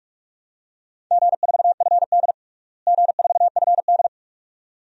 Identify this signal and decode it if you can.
Morse code, heard as a keyed tone, about 41 wpm, 700 Hz. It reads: G4FD G4FD
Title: DXCC-100 Call Signs - 4 Characters - Repeated 2x - 45wpm